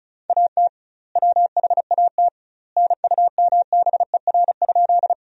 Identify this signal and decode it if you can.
Morse code, heard as a keyed tone, about 35 wpm, 700 Hz. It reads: AT WHAT NUMBER?
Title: 2x Word-Spacing - Sentences from Top 100 Words - 35wpm